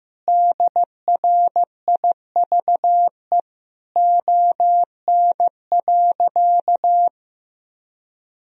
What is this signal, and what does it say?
Morse code, heard as a keyed tone, about 16 wpm, 700 Hz. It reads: DRIVE ON.
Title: Sentences from Top 1000 Words - Review Part 2 - 15wpm